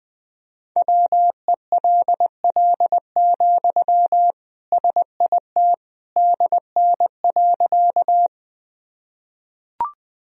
Morse code, heard as a keyed tone, about 20 wpm, 700 Hz. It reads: WELL, SIT DN.
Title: Sentences from Top 500 Words and Abbreviations 20wpm